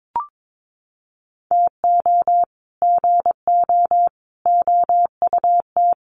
Morse code, heard as a keyed tone, about 22 wpm, 700 Hz. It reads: TO GO OUT
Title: Sets of 5 Words Encore2 22wpm